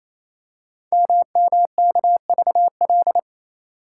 Morse code, heard as a keyed tone, about 28 wpm, 700 Hz. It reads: MMKVL
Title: Code Groups - 5 Letters - 28wpm